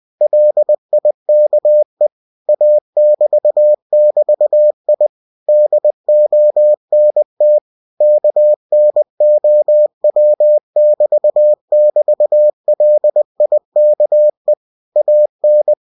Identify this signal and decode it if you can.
Morse code, heard as a keyed tone, about 20 wpm, 600 Hz. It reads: LIKE A<BT><BT>I DONT KNOW<BT><BT>LIKE AN
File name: cw-books / winnie-the-pooh / winnie-20wpm-page-0043